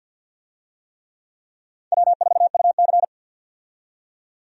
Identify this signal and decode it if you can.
Morse code, heard as a keyed tone, about 47 wpm, 700 Hz. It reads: W4UC
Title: USA Call Signs - 4 Characters - 50wpm